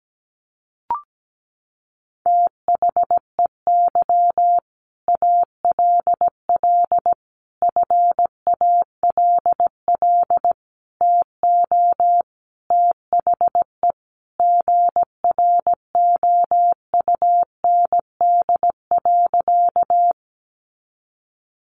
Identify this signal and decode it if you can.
Morse code, heard as a keyed tone, about 17 wpm, 700 Hz. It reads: THEY ALL FALL TO THE GROUND.
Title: Sentences from Top 500 Words 17wpm